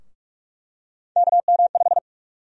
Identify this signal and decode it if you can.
Morse code, heard as a keyed tone, about 44 wpm, 700 Hz. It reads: KM5
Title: USA Call Sign Prefixes - Repeated 2x - 45wpm